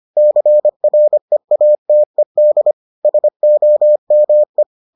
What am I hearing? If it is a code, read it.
Morse code, heard as a keyed tone, about 25 wpm, 600 Hz. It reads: CREATED SOME